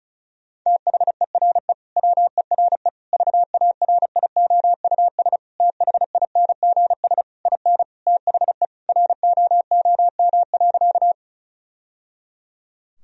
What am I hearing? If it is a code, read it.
Morse code, heard as a keyed tone, about 34 wpm, 700 Hz. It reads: THERE WERE VARIOUS THINGS IN THE ROOM.
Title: Sentences from Top 1000 Words - Review Part 4 - 35wpm